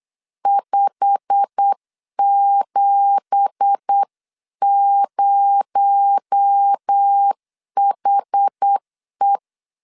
Morse code, heard as a keyed tone, about 8 wpm, 800 Hz. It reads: 570HE
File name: CW / DET / 1 9 1500C S A 8wpm 5637